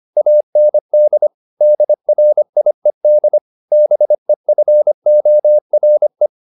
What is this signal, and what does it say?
Morse code, heard as a keyed tone, about 25 wpm, 600 Hz. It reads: AND DRIED BEFORE
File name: the old man and the sea part 1 25wpm